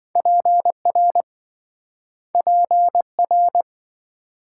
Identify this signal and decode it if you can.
Morse code, heard as a keyed tone, about 26 wpm, 700 Hz. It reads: PR PR